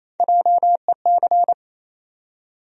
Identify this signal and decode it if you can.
Morse code, heard as a keyed tone, about 28 wpm, 700 Hz. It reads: JEC
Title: USA Call Sign Postfixes - 3 Letters 28wpm